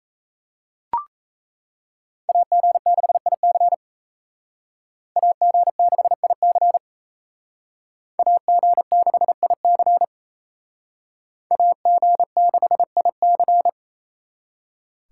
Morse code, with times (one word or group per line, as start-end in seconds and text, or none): 2.3-3.7 AG6IC
5.2-6.8 AG6IC
8.2-10.0 AG6IC
11.5-13.7 AG6IC